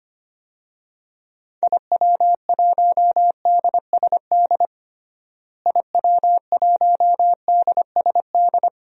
Morse code, heard as a keyed tone, about 25 wpm, 700 Hz. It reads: IW1DSD IW1DSD
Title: DXCC-100 Call Signs - 6 Characters - Repeated 2x - 25wpm